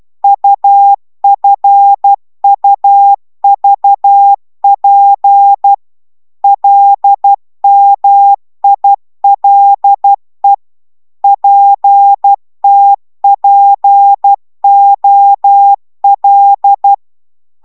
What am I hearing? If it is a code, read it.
Morse code, heard as a keyed tone, about 12 wpm, 800 Hz. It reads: UFUVP LMILE PTPOL